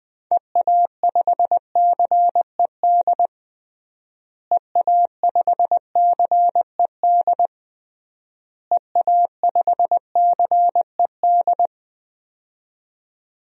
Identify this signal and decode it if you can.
Morse code, heard as a keyed tone, about 20 wpm, 700 Hz. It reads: EA5CED EA5CED EA5CED
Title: DXCC-100 Call Signs - 6 Characters - Repeated 3x - 20wpm